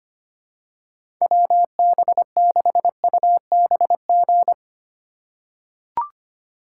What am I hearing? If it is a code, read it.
Morse code, heard as a keyed tone, about 25 wpm, 700 Hz. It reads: WB6UBG
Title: USA Call Signs - 6 Characters - 4x Character Spacing - 25wpm